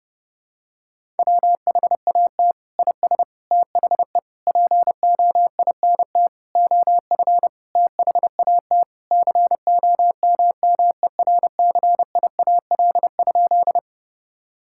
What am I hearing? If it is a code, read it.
Morse code, heard as a keyed tone, about 32 wpm, 700 Hz. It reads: WHAT IS THE POINT OF THAT COMMERCIAL?
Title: Sentences from Top 1000 Words - Review Part 5 - 30wpm